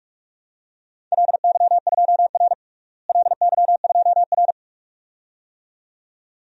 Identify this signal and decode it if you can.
Morse code, heard as a keyed tone, about 46 wpm, 700 Hz. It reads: LY2R LY2R
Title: DXCC-100 Call Signs - 4 Characters - Repeated 2x - 45wpm